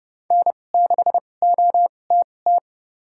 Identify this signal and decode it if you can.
Morse code, heard as a keyed tone, about 32 wpm, spaced as for 22 wpm, 700 Hz. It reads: N6OTT
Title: USA Call Signs - 5 Characters - 2x Character Spacing - 30wpm